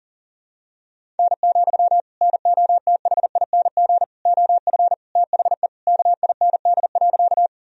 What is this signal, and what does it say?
Morse code, heard as a keyed tone, about 37 wpm, 700 Hz. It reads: N, NOTHING OF THE KIND.
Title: Sentences from Top 500 Words and Abbreviations 40wpm